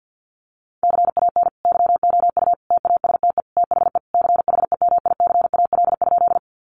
Morse code, heard as a keyed tone, about 50 wpm, 700 Hz. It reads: CAN YOU TASTE THE CHEMICAL?
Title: Sentences from Top 1200 Words 50wpm